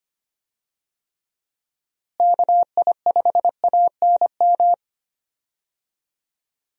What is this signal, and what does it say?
Morse code, heard as a keyed tone, about 25 wpm, 700 Hz. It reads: KI5ANM
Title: USA Call Signs - 6 Characters - Repeated 3x - 25wpm